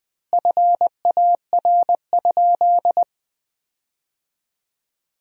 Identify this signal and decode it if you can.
Morse code, heard as a keyed tone, about 21 wpm, 700 Hz. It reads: FAR?
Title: 4x Word-Spacing - Sentences from Top 300 Words - 20wpm